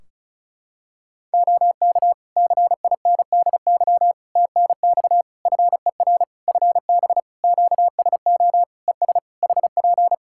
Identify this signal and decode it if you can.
Morse code, heard as a keyed tone, about 34 wpm, 700 Hz. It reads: OK CINDY TNX FER FB QSO ES HP
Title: K1USN SST QSO Practice 35wpm